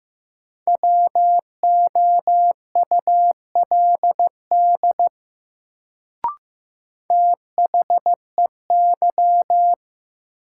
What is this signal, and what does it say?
Morse code, heard as a keyed tone, about 15 wpm, 700 Hz. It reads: WOULD THEY
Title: Sentences from Top 100 Words - New Word Drill - 15wpm